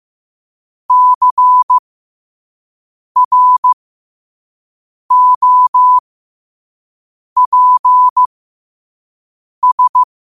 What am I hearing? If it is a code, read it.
Morse code, heard as a keyed tone, about 15 wpm, spaced as for 5 wpm, 1000 Hz. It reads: CROPS
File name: qotdfiles / 5wpm